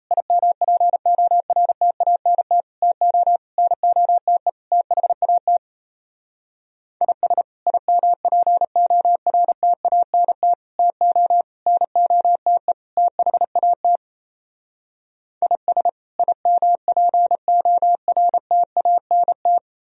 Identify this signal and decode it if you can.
Morse code, heard as a keyed tone, about 36 wpm, 700 Hz. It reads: IMPORTANT TO NOTE THAT IS IMPORTANT TO NOTE THAT IS IMPORTANT